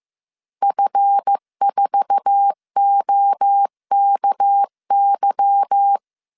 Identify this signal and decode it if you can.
Morse code, heard as a keyed tone, about 15 wpm, 800 Hz. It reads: F4OKY
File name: CW / DET / 1 9 1500C S C 14wpm 3245